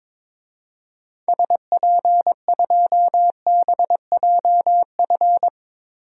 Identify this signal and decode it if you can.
Morse code, heard as a keyed tone, about 22 wpm, 700 Hz. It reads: SP2BJF
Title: DXCC-100 Call Signs - 6 Characters - 22wpm